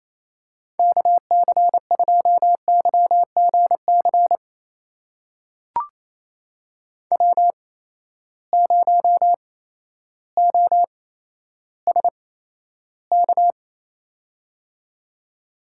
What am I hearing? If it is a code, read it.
Morse code, heard as a keyed tone, about 28 wpm, spaced as for 13 wpm, 700 Hz. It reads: KC2YGC W0OSK
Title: USA Call Signs - 8x Character Spacing - 28wpm